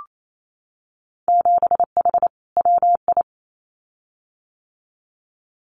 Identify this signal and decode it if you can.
Morse code, heard as a keyed tone, about 28 wpm, 700 Hz. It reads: 7H WI
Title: Winter Field Day Exchanges Encore 28wpm